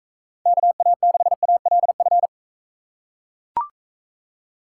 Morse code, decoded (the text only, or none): KA6ALF